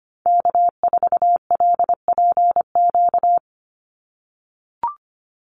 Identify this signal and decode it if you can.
Morse code, heard as a keyed tone, about 25 wpm, 690 Hz. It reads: K4LPQ
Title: NA SOTA Activator Callsigns 25wpm